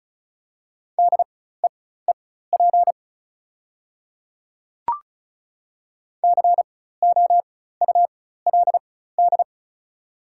Morse code, heard as a keyed tone, about 36 wpm, spaced as for 16 wpm, 690 Hz. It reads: DEEP COULD
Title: 4x Character Spacing - Top 500 Words - 35wpm